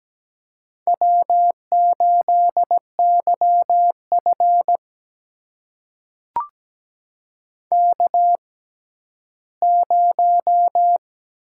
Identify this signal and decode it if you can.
Morse code, heard as a keyed tone, about 17 wpm, spaced as for 11 wpm, 700 Hz. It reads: W8YF K0
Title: USA Call Signs - 4 Characters - 6x Character Spacing - 17wpm